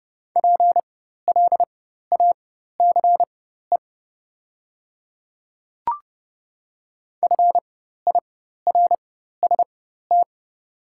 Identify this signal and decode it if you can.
Morse code, heard as a keyed tone, about 30 wpm, spaced as for 14 wpm, 690 Hz. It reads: PLACE FIRST
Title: 4x Character Spacing - Top 200 Words - 30wpm